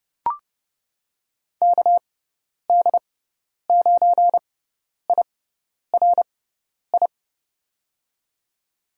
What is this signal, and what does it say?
Morse code, heard as a keyed tone, about 30 wpm, spaced as for 10 wpm, 700 Hz. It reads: KD9IRI
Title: USA Call Signs - 6x Character Spacing - 30wpm